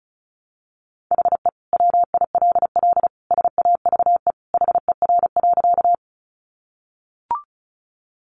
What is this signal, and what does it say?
Morse code, heard as a keyed tone, about 34 wpm, 700 Hz. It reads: HE WILL SAVE HER.
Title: Sentences from Top 1100 Words 35wpm